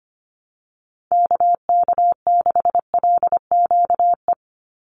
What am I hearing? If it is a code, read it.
Morse code, heard as a keyed tone, about 25 wpm, 700 Hz. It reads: KK6LQE
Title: USA Call Signs 25wpm